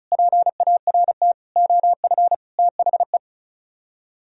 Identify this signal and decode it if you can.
Morse code, heard as a keyed tone, about 46 wpm, spaced as for 41 wpm, 700 Hz. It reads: PART OF THE
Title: Speed-Racing - Sets of 5 Words 35wpm